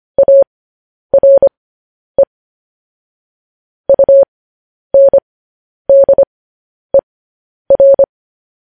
Morse code, heard as a keyed tone, about 25 wpm, spaced as for 10 wpm, 550 Hz. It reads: ARE UNDER
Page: News Headlines in Morse Code at 10 WPM